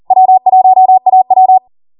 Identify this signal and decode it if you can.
Morse code, heard as a keyed tone, about 40 wpm, 750 Hz. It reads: W1AW